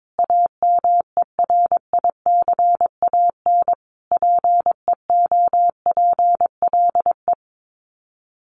Morse code, heard as a keyed tone, about 22 wpm, 700 Hz. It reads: AMERICAN PEOPLE
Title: Sets of 5 Words Encore 22wpm